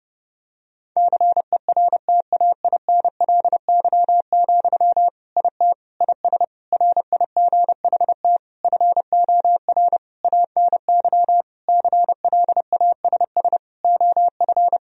Morse code, heard as a keyed tone, about 30 wpm, 700 Hz. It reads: CERTAINLY, IT IS RIGHT FOR ANY CLASS OF
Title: Sentences from Top 1000 Words - Review Part 2 - 30wpm